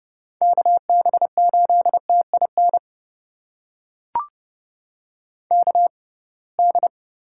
Morse code, heard as a keyed tone, about 30 wpm, spaced as for 23 wpm, 700 Hz. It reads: KB8TIN KD
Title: USA Call Signs - 6 Characters - 6x Character Spacing - 30wpm